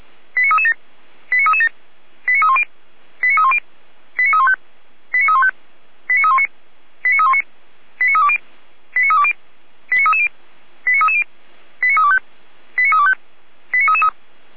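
Five-tone selective call (ZVEI), heard as quick runs of five tones, tones of about 70 ms. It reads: 89337 89337 89310 89310 89315 89315 89319 89319 89320 89320 89330 89330 89325 89325 89333
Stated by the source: ZVEI five-tone selective call received on an SDR.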